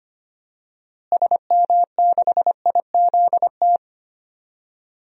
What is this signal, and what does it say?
Morse code, heard as a keyed tone, about 25 wpm, 700 Hz. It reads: SM6IZT
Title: DXCC-100 Call Signs - 6 Characters - Repeated 3x - 25wpm